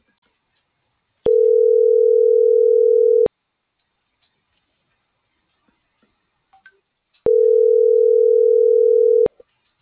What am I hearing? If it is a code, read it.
Ringback tone (North America), 440 + 480 Hz.
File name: CENTRAL TERRESTRE / out alberto.lazar.25 - .46005